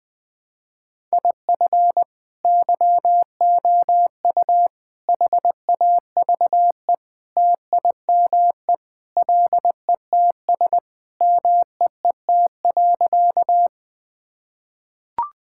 Morse code, heard as a keyed tone, about 20 wpm, 700 Hz. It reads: IF YOU HAVE TIME LETS MEET.
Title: Sentences with 3-Word Phrases - Spoken Beginning - 20wpm